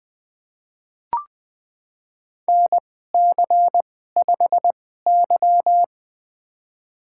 Morse code, heard as a keyed tone, about 20 wpm, spaced as for 14 wpm, 700 Hz. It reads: NC5Y
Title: USA Call Signs - 2x Character Spacing - 20wpm